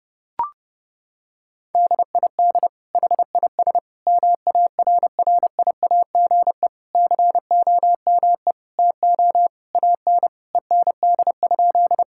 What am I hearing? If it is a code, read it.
Morse code, heard as a keyed tone, about 32 wpm, 700 Hz. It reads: DID HIS MARRIAGE COME TO AN END?
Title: Sentences from Top 1000 Words - Review Part 5 - 30wpm